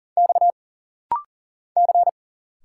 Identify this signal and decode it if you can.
Morse code, heard as a keyed tone, about 40 wpm, 700 Hz. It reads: X C
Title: Single Letter-Number Alt - Rapid-Fire 40wpm